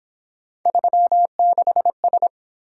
Morse code, heard as a keyed tone, about 27 wpm, 700 Hz. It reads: 36S
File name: Speed-Racing - Three Letter-Number Combinations - 22wpm